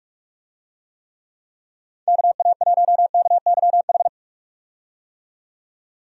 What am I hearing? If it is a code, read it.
Morse code, heard as a keyed tone, about 46 wpm, 700 Hz. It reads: KA1KYH
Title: USA Call Signs - 6 Characters - 45wpm